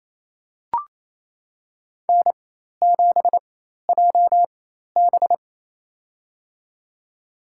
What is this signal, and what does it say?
Morse code, heard as a keyed tone, about 28 wpm, spaced as for 13 wpm, 700 Hz. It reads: N7JB
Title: USA Call Signs - 4x Character Spacing - 28wpm